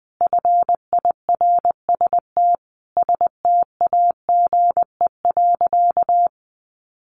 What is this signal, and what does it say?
Morse code, heard as a keyed tone, about 20 wpm, 700 Hz. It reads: FIRST STAGE.
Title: Sentences from Top 1200 Words 20wpm